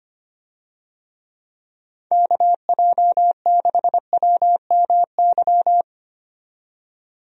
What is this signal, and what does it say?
Morse code, heard as a keyed tone, about 25 wpm, 700 Hz. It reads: KJ6WMY
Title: USA Call Signs - 6 Characters - 25wpm